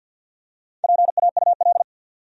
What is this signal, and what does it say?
Morse code, heard as a keyed tone, about 47 wpm, 700 Hz. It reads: PAUL
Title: USA Common Names - Max 5 Letters - 50wpm